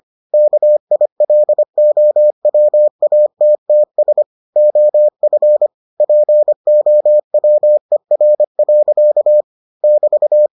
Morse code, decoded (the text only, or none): KILOWATTS OF POWER. <BT>